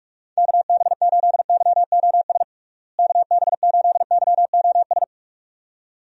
Morse code, heard as a keyed tone, about 46 wpm, 700 Hz. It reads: KB8YOS KB8YOS
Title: USA Call Signs - 6 Characters - Repeated 2x - 45wpm